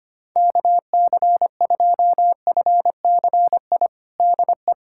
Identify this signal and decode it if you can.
Morse code, heard as a keyed tone, about 25 wpm, 700 Hz. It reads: KC2FCI DE
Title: USA Callsign DE Callsign 25wpm